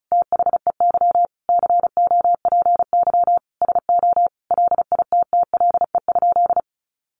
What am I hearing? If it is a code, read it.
Morse code, heard as a keyed tone, about 34 wpm, 700 Hz. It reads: THEY COPY SO LITTLE?
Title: Sentences from Top 1400 Words 35wpm